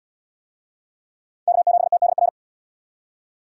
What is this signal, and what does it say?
Morse code, heard as a keyed tone, about 69 wpm, 700 Hz. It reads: K7TDR